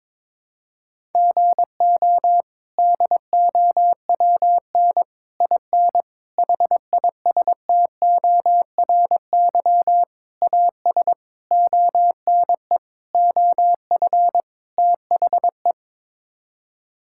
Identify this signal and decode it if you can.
Morse code, heard as a keyed tone, about 22 wpm, 700 Hz. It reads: GO DOWN IN HISTORY AS ONE OF THE
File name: practice / 22wpm / Sets of 8 Words - 22wpm